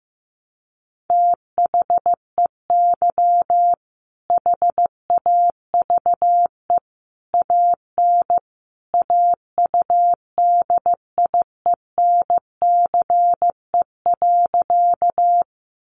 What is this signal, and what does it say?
Morse code, heard as a keyed tone, about 15 wpm, 700 Hz. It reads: THEY HAVE AN AUDIENCE.
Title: Sentences from Top 800 Words 15wpm